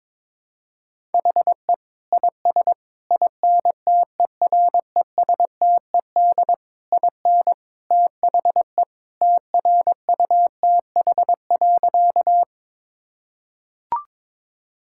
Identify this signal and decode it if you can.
Morse code, heard as a keyed tone, about 22 wpm, 700 Hz. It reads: HE IS INTERESTED IN THE TRUTH.
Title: Sentences from Top 1000 Words - Review Part 6 - 22wpm